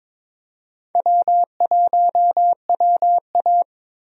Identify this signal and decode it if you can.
Morse code, heard as a keyed tone, about 22 wpm, 700 Hz. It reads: W1WA
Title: USA Call Signs - 4 Characters - 22wpm